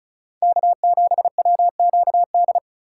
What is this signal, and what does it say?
Morse code, heard as a keyed tone, about 34 wpm, 700 Hz. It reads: K7WQD
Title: USA Call Signs - 5 Characters - 2x Character Spacing - 35wpm